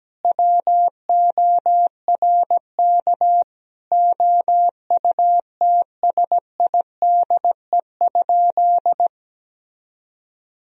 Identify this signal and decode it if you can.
Morse code, heard as a keyed tone, about 17 wpm, 700 Hz. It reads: WORK OUTSIDE?
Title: Sentences from Top 1000 Words - Review Part 4 - 17wpm